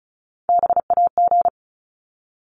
Morse code, heard as a keyed tone, about 34 wpm, 700 Hz. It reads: BAG
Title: Top 1501-1750 Words 35wpm